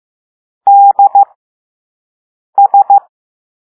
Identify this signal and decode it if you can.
Morse code, heard as a keyed tone, about 15 wpm, spaced as for 5 wpm, 800 Hz. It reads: DS